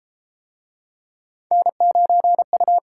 Morse code, heard as a keyed tone, about 32 wpm, 700 Hz. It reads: N9U